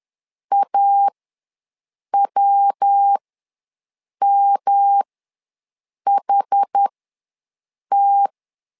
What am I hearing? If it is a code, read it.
Morse code, heard as a keyed tone, about 10 wpm, spaced as for 6 wpm, 800 Hz. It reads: AWMHT
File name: LEZIONI / 3 MWRO / 3 MW ws 10wpm 2213